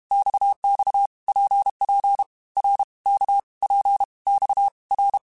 Morse code, heard as a keyed tone, about 32 wpm, 800 Hz. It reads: XX PP R K P X R
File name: Basic Letter Groups By7Groups / RKPX / RKPX@30